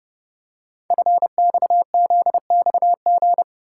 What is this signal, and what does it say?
Morse code, heard as a keyed tone, about 31 wpm, 690 Hz. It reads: FXZXG
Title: Code Groups - 5 Letters - 30wpm